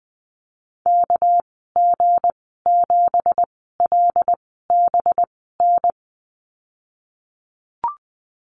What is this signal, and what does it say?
Morse code, heard as a keyed tone, about 20 wpm, spaced as for 15 wpm, 700 Hz. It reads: KG7LBN
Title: 219 Lesson 35g - Number 0 - Call Signs 20wpm